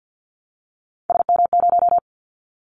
Morse code, heard as a keyed tone, about 50 wpm, 700 Hz. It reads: SM0
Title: Callsign Prefixes for Top 25 DXCC - 50wpm